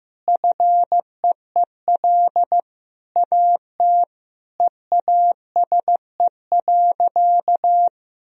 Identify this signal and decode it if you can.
Morse code, heard as a keyed tone, about 15 wpm, 700 Hz. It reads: FEEL AT EASE.
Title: Sentences from Top 1000 Words - Review Part 1 - 15wpm